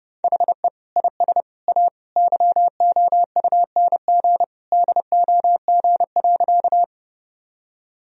Morse code, heard as a keyed tone, about 30 wpm, 700 Hz. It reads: HE IS A YOUNG DOG.
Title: Sentences from Top 1000 Words - Review Part 2 - 30wpm